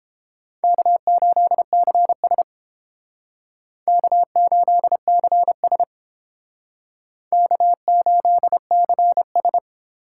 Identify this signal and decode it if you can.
Morse code, heard as a keyed tone, about 29 wpm, 700 Hz. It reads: K8CS K8CS K8CS